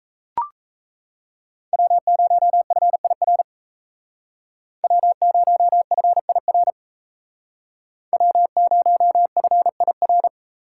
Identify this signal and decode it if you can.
Morse code, heard as a keyed tone, about 35 wpm, 700 Hz. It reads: W0FIR W0FIR W0FIR
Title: Speed Racing - USA 1x3 Call Signs 28wpm